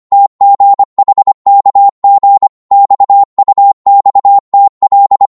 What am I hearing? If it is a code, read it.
Morse code, heard as a keyed tone, about 25 wpm, 800 Hz. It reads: TGHKG XUXTL